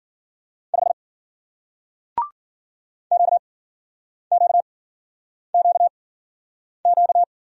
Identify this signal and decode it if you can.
Morse code, heard as a keyed tone, about 49 wpm, spaced as for 9 wpm, 700 Hz. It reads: 5 QQQQ
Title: Speed-Racing Single Letter-Number 40wpm